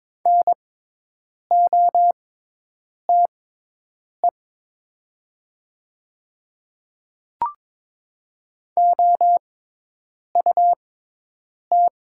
Morse code, heard as a keyed tone, about 22 wpm, spaced as for 8 wpm, 700 Hz. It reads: NOTE OUT
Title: 6x Character Spacing - Top 500 Words - 22wpm